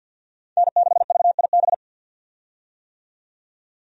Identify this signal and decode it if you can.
Morse code, heard as a keyed tone, about 53 wpm, 700 Hz. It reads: N6VIB
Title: USA Call Signs - 5 Characters - 50wpm